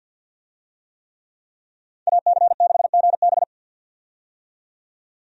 Morse code, heard as a keyed tone, about 52 wpm, 700 Hz. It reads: AC6GB
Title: USA Call Signs - 5 Characters - 50wpm